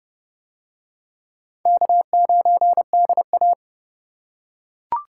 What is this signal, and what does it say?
Morse code, heard as a keyed tone, about 27 wpm, 690 Hz. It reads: K9DA